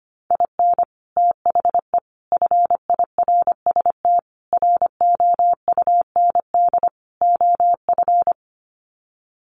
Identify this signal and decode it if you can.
Morse code, heard as a keyed tone, about 25 wpm, 700 Hz. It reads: IN THE FIRST ROUND OF
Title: Sets of 5 Words Encore 25wpm